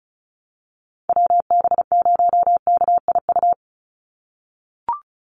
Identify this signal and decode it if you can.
Morse code, heard as a keyed tone, about 34 wpm, 700 Hz. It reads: WB0KIU
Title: NA SOTA Chaser Callsigns 35wpm